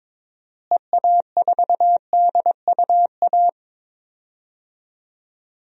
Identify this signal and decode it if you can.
Morse code, heard as a keyed tone, about 22 wpm, 700 Hz. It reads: EA4DUA
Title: DXCC-100 Call Signs - 6 Characters - Repeated 3x - 22wpm